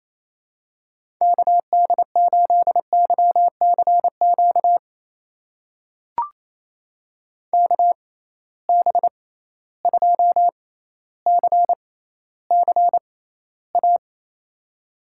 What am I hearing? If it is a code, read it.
Morse code, heard as a keyed tone, about 28 wpm, spaced as for 14 wpm, 700 Hz. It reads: KD8YCQ KB2CCA
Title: USA Call Signs - 6 Characters - 6x Character Spacing - 28wpm